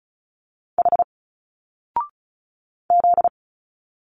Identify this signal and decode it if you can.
Morse code, heard as a keyed tone, about 35 wpm, 700 Hz. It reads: H Z
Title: Single Letters 35wpm